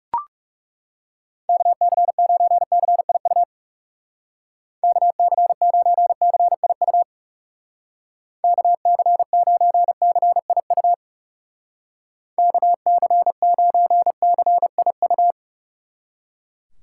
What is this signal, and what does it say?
Morse code, heard as a keyed tone, about 39 wpm, 700 Hz. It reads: KC9CIU KC9CIU KC9CIU KC9CIU